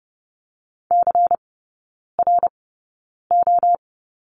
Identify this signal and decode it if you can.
Morse code, heard as a keyed tone, about 30 wpm, spaced as for 9 wpm, 700 Hz. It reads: CRO
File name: Lesson 16e - Letter c - Three Characters 30wpm